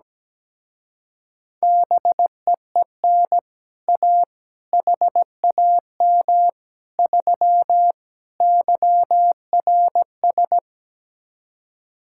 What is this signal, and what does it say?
Morse code, heard as a keyed tone, about 17 wpm, 700 Hz. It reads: BEEN A HAM 3 YRS
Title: Speed-Racing Common QSO Phrases Encore - 17wpm